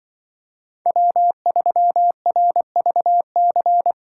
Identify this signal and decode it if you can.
Morse code, heard as a keyed tone, about 24 wpm, 700 Hz. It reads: W3RVC
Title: Speed Racing - USA 1x3 Call Signs 20wpm